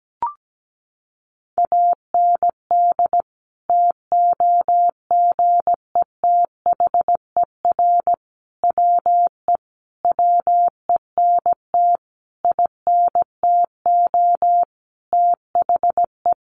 Morse code, heard as a keyed tone, about 17 wpm, 700 Hz. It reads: AND TOGETHER WE WENT INTO THE
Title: Sentences from Top 600 Words 17wpm